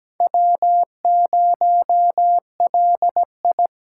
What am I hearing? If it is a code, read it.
Morse code, heard as a keyed tone, about 17 wpm, 700 Hz. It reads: W0LI
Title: USA Call Signs - 4 Characters - 2x Character Spacing - 17wpm